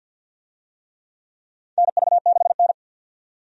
Morse code, heard as a keyed tone, about 51 wpm, 700 Hz. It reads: NV6N
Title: USA Call Signs - 4 Characters - 50wpm